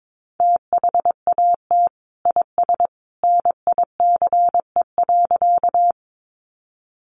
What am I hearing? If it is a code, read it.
Morse code, heard as a keyed tone, about 22 wpm, 700 Hz. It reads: THAT IS NICE.
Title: Sentences from Top 1100 Words 22wpm